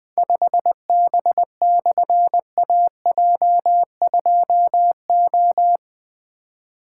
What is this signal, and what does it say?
Morse code, heard as a keyed tone, about 20 wpm, 700 Hz. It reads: 5B/AJ2O
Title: Difficult Call Signs 20wpm